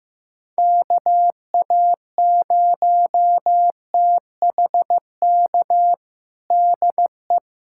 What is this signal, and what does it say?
Morse code, heard as a keyed tone, about 15 wpm, 700 Hz. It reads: KA0THK DE
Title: USA Callsign DE Callsign 15wpm